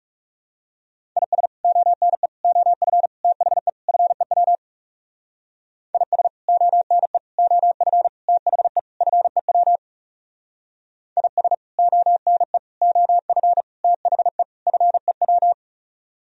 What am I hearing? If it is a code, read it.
Morse code, heard as a keyed tone, about 36 wpm, 700 Hz. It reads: IS ONE OF THE FEW IS ONE OF THE FEW IS ONE OF THE FEW
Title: Speed-Racing - Sets of 5 Words 30wpm